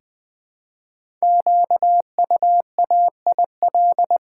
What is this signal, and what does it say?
Morse code, heard as a keyed tone, about 20 wpm, 700 Hz. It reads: QUAIL
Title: Difficult Words with Hints - 20wpm